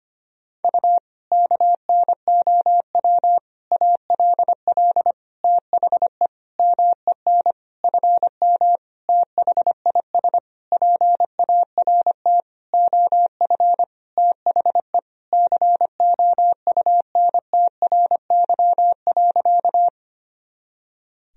Morse code, heard as a keyed tone, about 25 wpm, 700 Hz. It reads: U KNOW ALL THE MEN FM THIS PART OF THE COUNTRY.
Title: Sentences from Top 300 Words and Abbreviations 25wpm